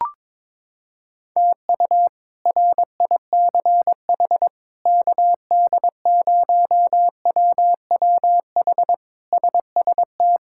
Morse code, heard as a keyed tone, about 22 wpm, 700 Hz. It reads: TU RICH KD0WWH SST
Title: K1USN SST QSO Practice - 22wpm